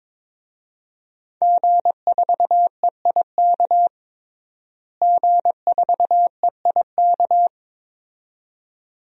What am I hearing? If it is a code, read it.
Morse code, heard as a keyed tone, about 22 wpm, 700 Hz. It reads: G4EIK G4EIK
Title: DXCC-100 Call Signs - 5 Characters - Repeated 2x - 22wpm